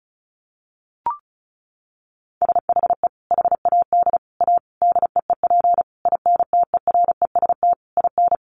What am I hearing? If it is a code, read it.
Morse code, heard as a keyed tone, about 34 wpm, 700 Hz. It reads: SHE HAD A DEEP INTEREST IN
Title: Sentences from Top 1300 Words 35wpm